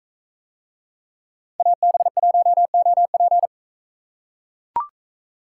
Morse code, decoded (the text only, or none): AB1OP